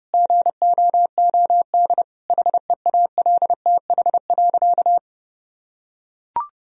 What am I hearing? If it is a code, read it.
Morse code, heard as a keyed tone, about 29 wpm, 700 Hz. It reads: GOOD HEALTH.
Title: Sentences from Top 1000 Words - Review Part 3 - 30wpm